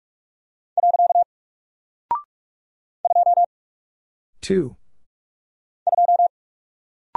Morse code, heard as a keyed tone, about 44 wpm, 700 Hz. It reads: . 2 2
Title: Single Character (Letter:Number:Punctuation:Prosign) Alt 45wpm